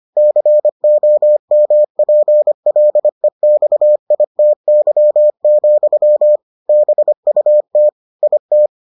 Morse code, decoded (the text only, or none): COMPLEXITY, BUT IT